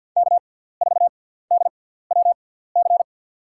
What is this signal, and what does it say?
Morse code, heard as a keyed tone, about 52 wpm, spaced as for 17 wpm, 700 Hz. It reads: K4DWC
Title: USA Call Signs - 6x Character Spacing - 50wpm